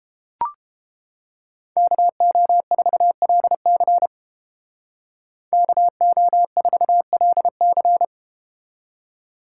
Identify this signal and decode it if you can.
Morse code, heard as a keyed tone, about 30 wpm, 700 Hz. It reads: KO4LC KO4LC